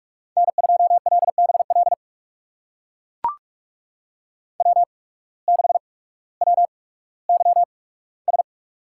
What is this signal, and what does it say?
Morse code, heard as a keyed tone, about 44 wpm, spaced as for 19 wpm, 700 Hz. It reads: N2LBL W6WYS
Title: USA Call Signs - 8x Character Spacing - 45wpm